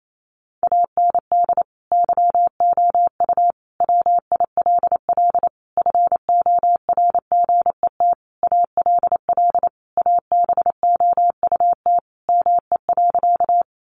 Morse code, heard as a keyed tone, about 28 wpm, 700 Hz. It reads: AND YOU WILL FORGET ALL ABOUT ME.
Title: Sentences from Top 1200 Words 28wpm